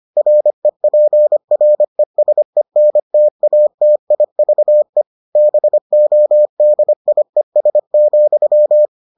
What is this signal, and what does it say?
Morse code, heard as a keyed tone, about 25 wpm, 600 Hz. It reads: REPRESENTATIVE BODIES,